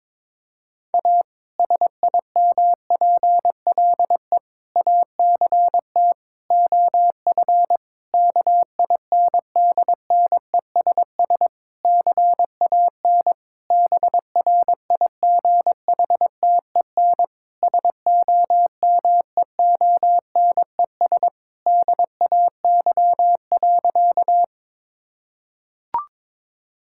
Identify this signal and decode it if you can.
Morse code, heard as a keyed tone, about 22 wpm, 700 Hz. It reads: A SIMPLE ACT OF KINDNESS CAN BRIGHTEN SOMEONES DAY.
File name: Difficult Sentences with Hints - 22wpm